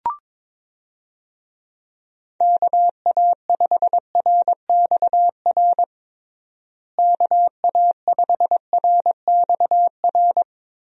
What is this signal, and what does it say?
Morse code, heard as a keyed tone, about 22 wpm, 700 Hz. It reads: KA5RXR KA5RXR
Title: USA Call Signs - 6 Characters - Repeated 3x - 22wpm